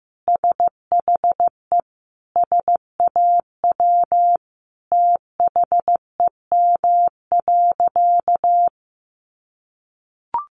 Morse code, heard as a keyed tone, about 15 wpm, 700 Hz. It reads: SHE SAW THEM.
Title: Sentences from Top 1300 Words 15wpm